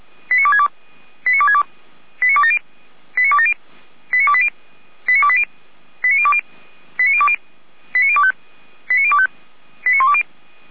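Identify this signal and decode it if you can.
Five-tone selective call (ZVEI), heard as quick runs of five tones, tones of about 70 ms. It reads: 89262 89262 89270 89270 89280 89280 89020 89020 89025 89025 89120
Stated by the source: ZVEI five-tone selective call 89358 received on an SDR.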